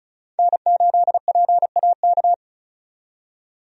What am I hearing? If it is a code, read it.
Morse code, heard as a keyed tone, about 36 wpm, 700 Hz. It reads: N8PAK
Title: USA Call Signs - 5 Characters - 35wpm